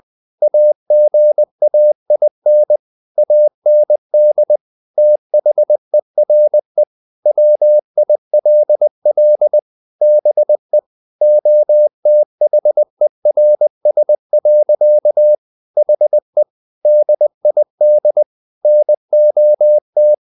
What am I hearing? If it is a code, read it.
Morse code, heard as a keyed tone, about 20 wpm, 600 Hz. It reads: AGAIN AND THERE WILL BE OTHERS. HE DID NOT
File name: the old man and the sea part 2 20wpm